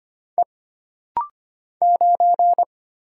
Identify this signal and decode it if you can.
Morse code, heard as a keyed tone, about 25 wpm, 700 Hz. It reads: E 9
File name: Single Character (Letter:Number:Punctuation:Prosign) - Rapid-Fire - 25wpm